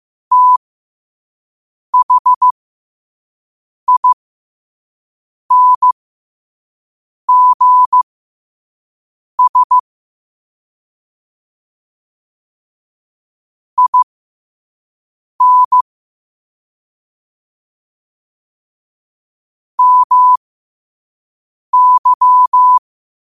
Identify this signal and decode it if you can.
Morse code, heard as a keyed tone, about 15 wpm, spaced as for 5 wpm, 1000 Hz. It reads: THINGS IN MY
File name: qotdfiles / 5wpm